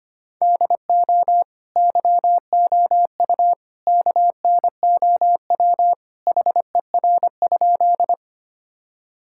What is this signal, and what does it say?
Morse code, heard as a keyed tone, about 25 wpm, 700 Hz. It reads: DO YOU KNOW HER?
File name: practice / 25wpm / Sentences with 3-Word Phrases - 25wpm